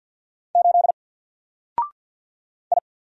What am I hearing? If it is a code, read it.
Morse code, heard as a keyed tone, about 48 wpm, 700 Hz. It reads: 8 I
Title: Single Letter-Number Alt 50wpm